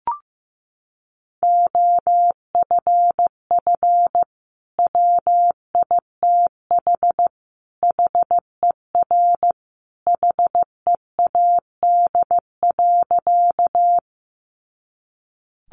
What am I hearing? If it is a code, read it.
Morse code, heard as a keyed tone, about 15 wpm, 700 Hz. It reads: OFF WITH HER HEAD.
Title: Sentences from Top 300 Words 15wpm